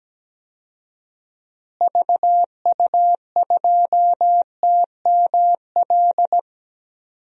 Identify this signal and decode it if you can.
Morse code, heard as a keyed tone, about 17 wpm, 700 Hz. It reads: VU2TML
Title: DXCC-100 Call Signs - 6 Characters - Repeated 2x - 17wpm